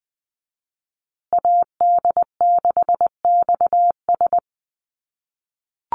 Morse code, heard as a keyed tone, about 20 wpm, 700 Hz. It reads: AD6XS
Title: USA Call Signs Repeated 2x 20wpm